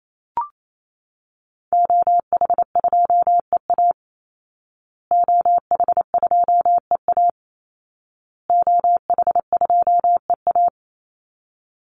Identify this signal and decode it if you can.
Morse code, heard as a keyed tone, about 28 wpm, 700 Hz. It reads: OH2EA OH2EA OH2EA
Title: Call Signs Repeated 2x - 28wpm